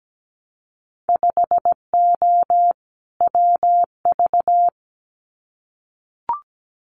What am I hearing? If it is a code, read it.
Morse code, heard as a keyed tone, about 17 wpm, 700 Hz. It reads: 5O WV
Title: Winter Field Day Exchanges Encore 17wpm